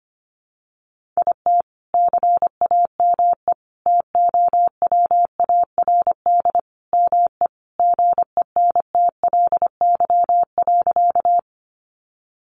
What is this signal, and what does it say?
Morse code, heard as a keyed tone, about 25 wpm, 700 Hz. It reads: IT CAME TOWARD ME GENTLY.
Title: Sentences from Top 1500 Words 25wpm